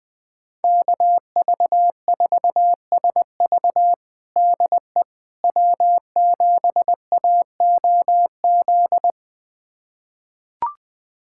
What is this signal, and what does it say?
Morse code, heard as a keyed tone, about 21 wpm, 700 Hz. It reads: KV4SV DE W7AOZ
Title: USA Callsign DE Callsign - 5 Characters - 20wpm